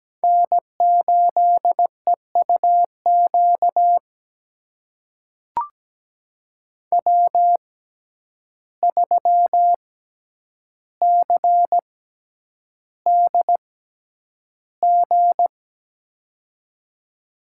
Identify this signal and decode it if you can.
Morse code, heard as a keyed tone, about 17 wpm, spaced as for 9 wpm, 700 Hz. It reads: N8EUQ W3CDG
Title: USA Call Signs - 5 Characters - 6x Character Spacing - 17wpm